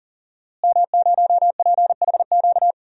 Morse code, heard as a keyed tone, about 40 wpm, 700 Hz. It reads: M0PHQ
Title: DXCC-100 Call Signs - 5 Characters - Repeated 3x - 40wpm